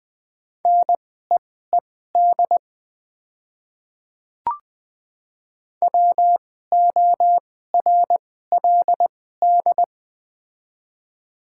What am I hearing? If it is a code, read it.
Morse code, heard as a keyed tone, about 20 wpm, spaced as for 14 wpm, 700 Hz. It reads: NEED WORLD
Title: 2x Character Spacing - Top 300 Words - 20wpm